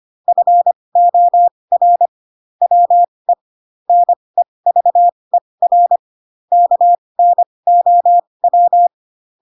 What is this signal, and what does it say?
Morse code, heard as a keyed tone, about 25 wpm, spaced as for 20 wpm, 700 Hz. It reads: FOR WE NEVER KNOW
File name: Pinocchio / Pinocchio ch7 15 25 ik1hge